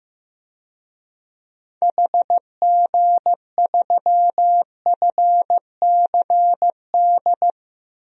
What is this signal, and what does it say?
Morse code, heard as a keyed tone, about 15 wpm, 700 Hz. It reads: HG3FCD
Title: DXCC-100 Call Signs - 6 Characters - Repeated 2x - 15wpm